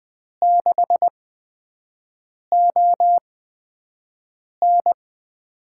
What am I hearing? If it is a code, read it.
Morse code, heard as a keyed tone, about 20 wpm, spaced as for 5 wpm, 700 Hz. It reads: 6ON
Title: USA Call Signs - 5 Characters - 8x Character Spacing - 20wpm